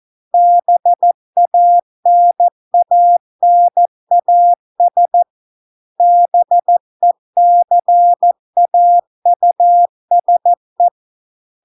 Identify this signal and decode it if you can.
Morse code, heard as a keyed tone, about 14 wpm, 700 Hz. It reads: BANANAS BECAUSE